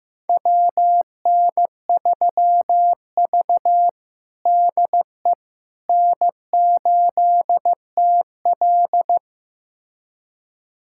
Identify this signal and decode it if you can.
Morse code, heard as a keyed tone, about 15 wpm, 700 Hz. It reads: WN3V DE N8TL
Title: USA Callsign DE Callsign - 4 Characters - 15wpm